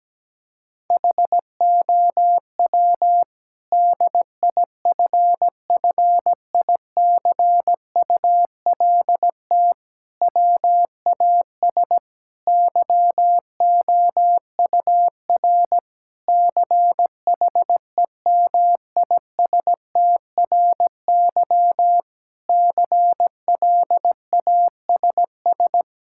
Morse code, heard as a keyed tone, about 17 wpm, 700 Hz. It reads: HOW DIFFICULT WAS YOUR CHEMISTRY CLASS
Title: Sentences from Top 1000 Words - Review Part 6 - 17wpm